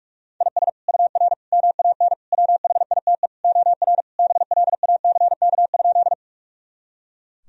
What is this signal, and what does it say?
Morse code, heard as a keyed tone, about 46 wpm, 700 Hz. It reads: IS UR MAN WHITE OR BLACK?
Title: Sentences from Top 400 Words and Abbreviations 45wpm